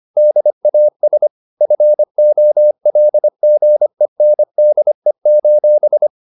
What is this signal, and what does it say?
Morse code, heard as a keyed tone, about 25 wpm, 600 Hz. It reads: DAS FOLGENDE: